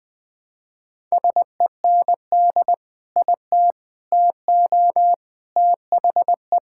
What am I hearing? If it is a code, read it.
Morse code, heard as a keyed tone, about 21 wpm, 700 Hz. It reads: SEND IT TO THE
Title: Sentences with 3-Word Phrases - 20wpm